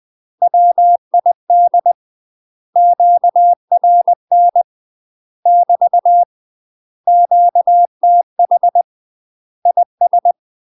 Morse code, heard as a keyed tone, about 20 wpm, 700 Hz. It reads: WID QRN <BT> QTH IS
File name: UK MASTER QSOs TRAINING / 20WPM / UK MASTER QSOs 0070